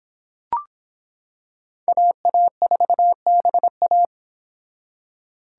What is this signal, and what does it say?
Morse code, heard as a keyed tone, about 26 wpm, 700 Hz. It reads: AA4BA